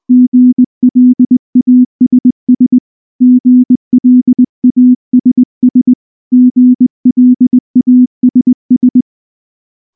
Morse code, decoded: GLASS GLASS GLASS